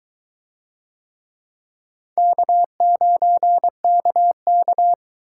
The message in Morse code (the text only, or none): K9KK